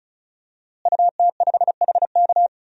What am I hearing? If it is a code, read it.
Morse code, heard as a keyed tone, about 36 wpm, 700 Hz. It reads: UT5HK
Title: DXCC-100 Call Signs - 5 Characters - Repeated 2x - 35wpm